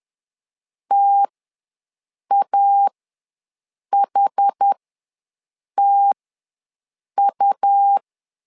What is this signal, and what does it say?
Morse code, heard as a keyed tone, about 10 wpm, spaced as for 6 wpm, 800 Hz. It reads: TAHTU
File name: LEZIONI / 2 UHDN / 2 UHD ws 10wpm 2058